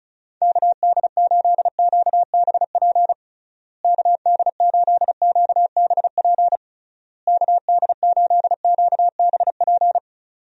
Morse code, heard as a keyed tone, about 34 wpm, 700 Hz. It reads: KD8QBP KD8QBP KD8QBP
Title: USA Call Signs - 6 Characters - Repeated 3x - 35wpm